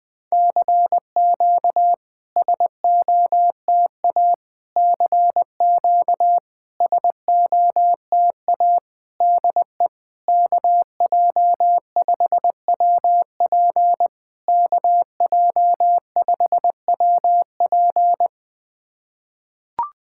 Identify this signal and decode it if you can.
Morse code, heard as a keyed tone, about 20 wpm, 690 Hz. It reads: CQ SOTA CQ SOTA DE KJ5WP KJ5WP
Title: SOTA QSO Practice 20wpm